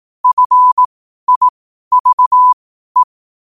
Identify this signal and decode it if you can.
Morse code, heard as a keyed tone, about 18 wpm, spaced as for 12 wpm, 1000 Hz. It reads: FIVE
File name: qotdfiles / 12wpm